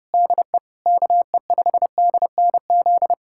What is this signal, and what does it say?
Morse code, heard as a keyed tone, about 31 wpm, 700 Hz. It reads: DE KE5DNZ
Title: USA Callsign DE Callsign - 6 Characters - 30wpm